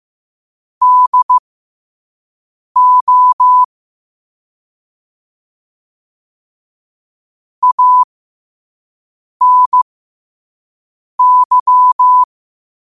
Morse code, heard as a keyed tone, about 15 wpm, spaced as for 5 wpm, 1000 Hz. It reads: DO ANY
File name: qotdfiles / 5wpm